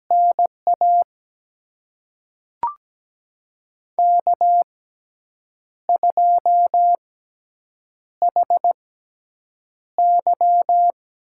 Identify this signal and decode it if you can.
Morse code, heard as a keyed tone, about 17 wpm, spaced as for 7 wpm, 700 Hz. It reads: NA K2HY
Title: USA Call Signs - 5 Characters - 6x Character Spacing - 17wpm